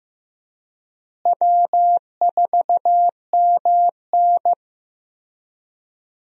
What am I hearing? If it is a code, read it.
Morse code, heard as a keyed tone, about 15 wpm, 700 Hz. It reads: W4MN
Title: USA Call Signs - 4 Characters - Repeated 3x - 15wpm